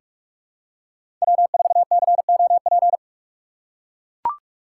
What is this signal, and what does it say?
Morse code, heard as a keyed tone, about 44 wpm, 700 Hz. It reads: W4COP
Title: USA Call Signs - 5 Characters - 6x Character Spacing - 45wpm